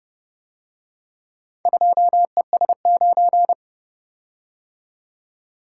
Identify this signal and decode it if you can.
Morse code, heard as a keyed tone, about 30 wpm, 700 Hz. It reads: 2ES9